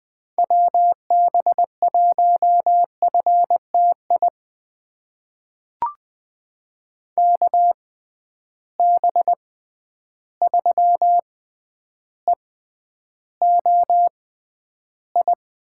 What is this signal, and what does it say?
Morse code, heard as a keyed tone, about 20 wpm, spaced as for 10 wpm, 700 Hz. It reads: WB1FTI KB3EOI
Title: USA Call Signs - 6 Characters - 6x Character Spacing - 20wpm